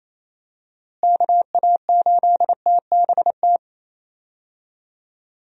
Morse code, heard as a keyed tone, about 27 wpm, 700 Hz. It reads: KA8TBT
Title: USA Call Signs - 6 Characters - Repeated 2x - 28wpm